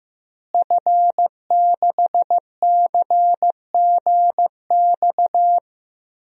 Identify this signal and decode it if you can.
Morse code, heard as a keyed tone, about 15 wpm, 700 Hz. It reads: F6CGX
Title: DXCC-100 Call Signs - 5 Characters - Repeated 3x - 15wpm